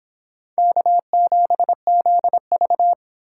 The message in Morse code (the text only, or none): K7ZV